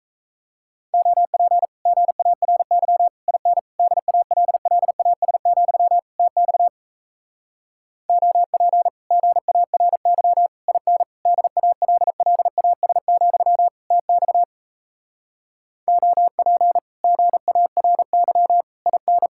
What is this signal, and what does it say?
Morse code, heard as a keyed tone, about 37 wpm, 700 Hz. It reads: OP GARY IN DALLAS, TX OP GARY IN DALLAS, TX OP GARY IN